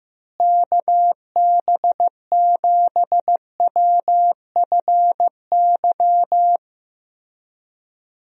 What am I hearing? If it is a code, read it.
Morse code, heard as a keyed tone, about 15 wpm, 700 Hz. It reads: KB7WFY
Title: USA Call Signs - 6 Characters - Repeated 2x - 15wpm